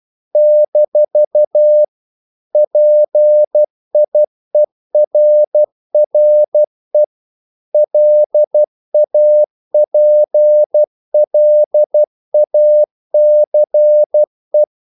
Morse code, heard as a keyed tone, about 12 wpm, 600 Hz. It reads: - PIERRE LAPLACE
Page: English quotations in CW (telegraphy - morse)